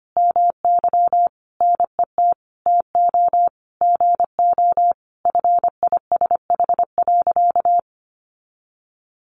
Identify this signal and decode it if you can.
Morse code, heard as a keyed tone, about 25 wpm, 700 Hz. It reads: MY NET TO GO FISH.
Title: Sentences from Top 1400 Words 25wpm